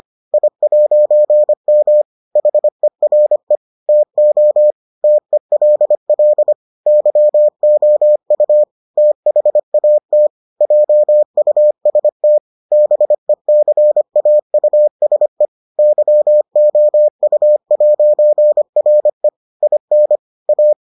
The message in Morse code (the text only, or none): I'M HERE TO TELL YOU THAT JUST BECAUSE YOU'RE IN A